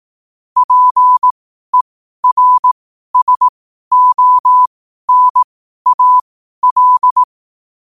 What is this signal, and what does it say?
Morse code, heard as a keyed tone, about 18 wpm, spaced as for 12 wpm, 1000 Hz. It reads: PERSONAL
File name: qotdfiles / 12wpm